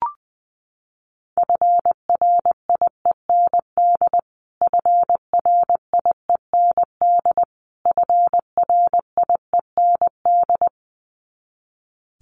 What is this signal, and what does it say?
Morse code, heard as a keyed tone, about 20 wpm, 700 Hz. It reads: FRIEND FRIEND FRIEND
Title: Top 500 Words Repeated 20wpm